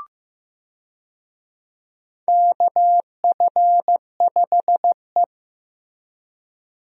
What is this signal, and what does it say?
Morse code, heard as a keyed tone, about 15 wpm, 700 Hz. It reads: KF5E
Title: USA Call Signs - 4 Characters - 15wpm